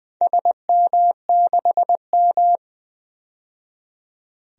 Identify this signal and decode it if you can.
Morse code, heard as a keyed tone, about 20 wpm, 700 Hz. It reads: SM6M
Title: DXCC-100 Call Signs - 4 Characters - 20wpm